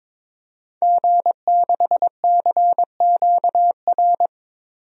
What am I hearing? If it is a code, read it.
Morse code, heard as a keyed tone, about 22 wpm, 700 Hz. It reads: G6CQR
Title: DXCC-100 Call Signs - 5 Characters - 22wpm